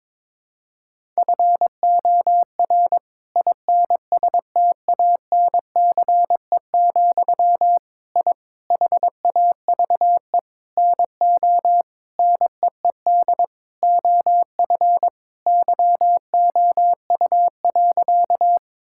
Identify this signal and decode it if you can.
Morse code, heard as a keyed tone, about 22 wpm, 700 Hz. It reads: FOR INSTANCE, I HAVE NO NEED OF YOU.
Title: Sentences from Top 1000 Words - Review Part 5 - 22wpm